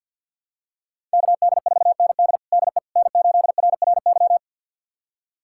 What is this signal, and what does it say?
Morse code, heard as a keyed tone, about 50 wpm, 700 Hz. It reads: KD4ND DE N8RRY
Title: USA Callsign DE Callsign - 5 Characters - 50wpm